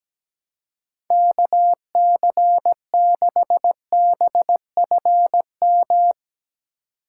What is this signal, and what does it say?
Morse code, heard as a keyed tone, about 17 wpm, 700 Hz. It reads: KC6BFM
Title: USA Call Signs - 6 Characters - 4x Character Spacing - 17wpm